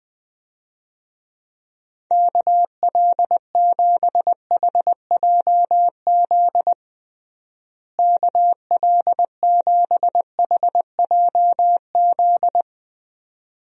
Morse code, heard as a keyed tone, about 21 wpm, 700 Hz. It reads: KL7HJZ KL7HJZ
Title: DXCC-100 Call Signs - 6 Characters - Repeated 2x - 20wpm